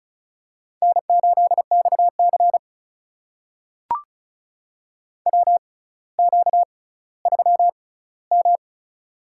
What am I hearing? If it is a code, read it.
Morse code, heard as a keyed tone, about 36 wpm, spaced as for 18 wpm, 690 Hz. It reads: N8XC WQ3M
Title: USA Call Signs - 4 Characters - 6x Character Spacing - 35wpm